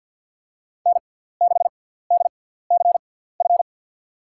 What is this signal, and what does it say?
Morse code, heard as a keyed tone, about 50 wpm, spaced as for 17 wpm, 700 Hz. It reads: N6DCF
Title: USA Call Signs - 5 Characters - 6x Character Spacing - 50wpm